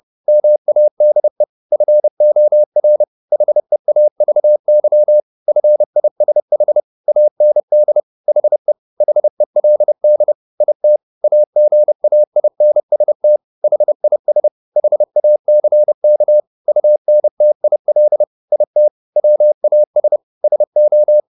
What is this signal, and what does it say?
Morse code, heard as a keyed tone, about 30 wpm, 600 Hz. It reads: MADE FOR HEAVY FISH AND HE HELD IT AGAINST HIS HACK UNTIL IT WAS SO
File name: the old man and the sea / the old man and the sea 1 30wpm